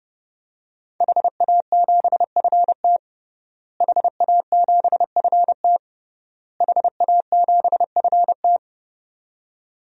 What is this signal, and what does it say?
Morse code, heard as a keyed tone, about 31 wpm, 700 Hz. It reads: HA7FT HA7FT HA7FT
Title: DXCC-100 Call Signs - 5 Characters - Repeated 3x - 30wpm